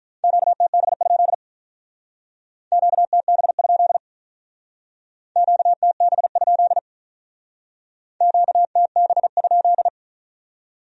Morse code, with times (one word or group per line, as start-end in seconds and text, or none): 0.2-1.3 QTB?
2.7-4.0 QTB?
5.4-6.8 QTB?
8.2-9.9 QTB?